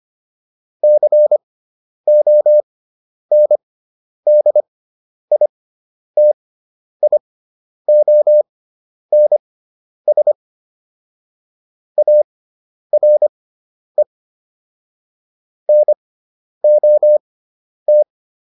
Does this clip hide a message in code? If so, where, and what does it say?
Yes: Morse code from 0.8 to 18.0 s, heard as a keyed tone, about 25 wpm, spaced as for 10 wpm, 600 Hz. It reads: CONDITIONS ARE NOT